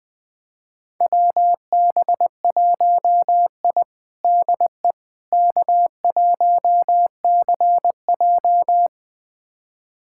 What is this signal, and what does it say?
Morse code, heard as a keyed tone, about 21 wpm, 700 Hz. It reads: WB1I DE K1CJ
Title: USA Callsign DE Callsign - 4 Characters - 20wpm